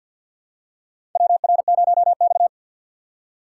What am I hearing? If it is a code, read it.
Morse code, heard as a keyed tone, about 47 wpm, 700 Hz. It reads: WR0X
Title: USA Call Signs - 4 Characters - 2x Character Spacing - 50wpm